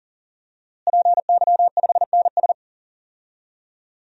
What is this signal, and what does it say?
Morse code, heard as a keyed tone, about 37 wpm, 700 Hz. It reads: PY5NS
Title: DXCC-100 Call Signs - 5 Characters - Repeated 3x - 40wpm